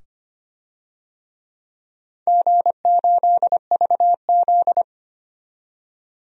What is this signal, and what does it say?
Morse code, heard as a keyed tone, about 25 wpm, 700 Hz. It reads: G8VZ
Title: DXCC-100 Call Signs - 4 Characters - 25wpm